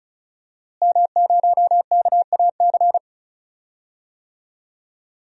Morse code, heard as a keyed tone, about 36 wpm, 700 Hz. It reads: M0KAC
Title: DXCC-100 Call Signs - 5 Characters - Repeated 3x - 35wpm